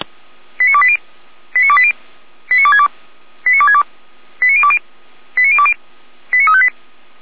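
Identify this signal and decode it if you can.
Five-tone selective call (ZVEI), heard as quick runs of five tones, tones of about 70 ms. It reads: 89280 89280 89262 89262 89020 89020 89358
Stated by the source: ZVEI five-tone call received on an SDR, from the server of Feuerwehr Orlamünde, a fire brigade.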